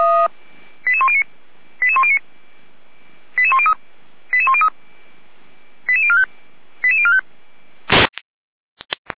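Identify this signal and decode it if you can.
Five-tone selective call (ZVEI), heard as quick runs of five tones, tones of about 70 ms. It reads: 88109 88109 88103 88103 88045 88045
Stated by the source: Five-tone ZVEI selective call received on SDR, recorded by fire brigade Feuerwehr Orlamünde.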